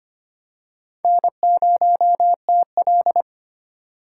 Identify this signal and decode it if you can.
Morse code, heard as a keyed tone, about 25 wpm, 700 Hz. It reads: N0TL
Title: USA Call Signs - 4 Characters - 4x Character Spacing - 25wpm